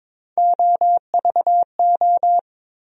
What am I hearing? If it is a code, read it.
Morse code, heard as a keyed tone, about 22 wpm, 700 Hz. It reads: OVO
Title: USA Call Sign Postfixes - 3 Letters 22wpm